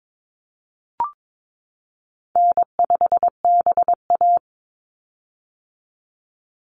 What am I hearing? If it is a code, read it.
Morse code, heard as a keyed tone, about 22 wpm, 700 Hz. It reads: N5BA
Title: Easy Call Signs - 22wpm